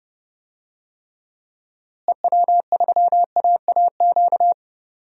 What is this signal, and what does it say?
Morse code, heard as a keyed tone, about 30 wpm, 700 Hz. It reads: EW3AAQ
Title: DXCC-100 Call Signs - 6 Characters - Repeated 2x - 30wpm